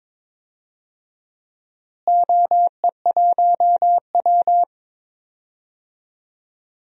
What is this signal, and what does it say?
Morse code, heard as a keyed tone, about 22 wpm, 700 Hz. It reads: OE1W
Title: DXCC-100 Call Signs - 4 Characters - Repeated 2x - 22wpm